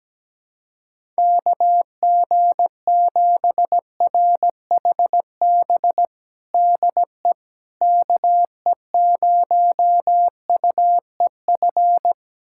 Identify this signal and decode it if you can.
Morse code, heard as a keyed tone, about 17 wpm, 700 Hz. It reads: KG7RHB DE KE0UEF
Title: USA Callsign DE Callsign - 6 Characters - 17wpm